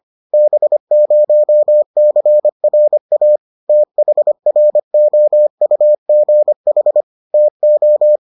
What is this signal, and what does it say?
Morse code, heard as a keyed tone, about 25 wpm, 600 Hz. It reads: B0CRA THROUGH TO